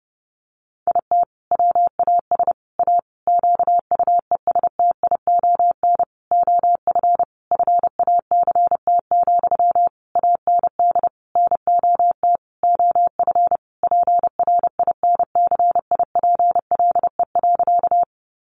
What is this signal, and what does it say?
Morse code, heard as a keyed tone, about 30 wpm, 700 Hz. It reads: IT WAS A QUESTION OF FACT, AND NOT OF PRINCIPLE.
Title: Sentences from Top 1400 Words 30wpm